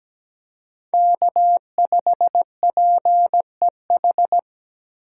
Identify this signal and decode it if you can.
Morse code, heard as a keyed tone, about 17 wpm, 700 Hz. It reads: K5PEH
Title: USA Call Signs - 5 Characters - 4x Character Spacing - 17wpm